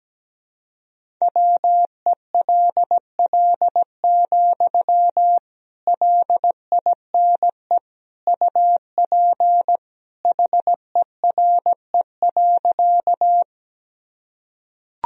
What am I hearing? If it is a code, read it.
Morse code, heard as a keyed tone, about 17 wpm, 700 Hz. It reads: WELL, LINE UP HERE.
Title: Sentences from Top 1000 Words - Review Part 1 - 17wpm